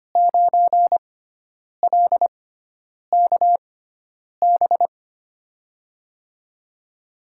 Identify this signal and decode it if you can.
Morse code, heard as a keyed tone, about 25 wpm, spaced as for 9 wpm, 700 Hz. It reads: 9LKB
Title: USA Call Signs - 6 Characters - 6x Character Spacing - 25wpm